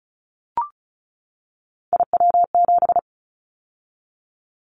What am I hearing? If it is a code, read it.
Morse code, heard as a keyed tone, about 34 wpm, 700 Hz. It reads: IW7
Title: Callsign Prefixes for Top 25 DXCC - 35wpm